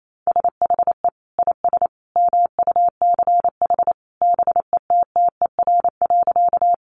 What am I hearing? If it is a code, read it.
Morse code, heard as a keyed tone, about 28 wpm, 700 Hz. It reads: SHE IS MUCH BETTER.
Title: Sentences from Top 400 Words 28wpm